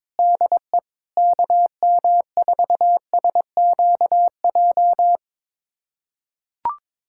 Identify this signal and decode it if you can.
Morse code, heard as a keyed tone, about 22 wpm, 700 Hz. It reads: DE KM4SQJ
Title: USA Callsign DE Callsign - 6 Characters - 22wpm